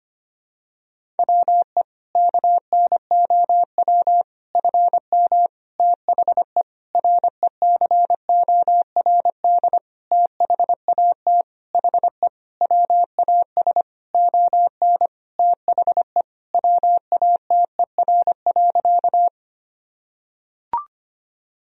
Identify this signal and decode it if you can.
Morse code, heard as a keyed tone, about 25 wpm, 700 Hz. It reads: WE KNOW FM THE RECORD THAT HE WAS ON THE WATER.
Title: Sentences from Top 500 Words and Abbreviations 25wpm